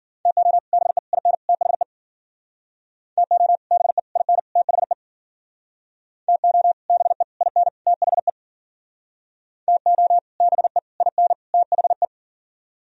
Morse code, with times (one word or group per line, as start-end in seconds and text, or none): 0.3-0.6 TO
0.7-1.0 BE
1.1-1.3 IN
1.5-1.8 THE
3.2-3.6 TO
3.7-4.0 BE
4.2-4.4 IN
4.6-4.9 THE
6.3-6.7 TO
6.9-7.2 BE
7.4-7.7 IN
7.9-8.3 THE
9.7-10.2 TO
10.4-10.8 BE
11.0-11.3 IN
11.5-12.0 THE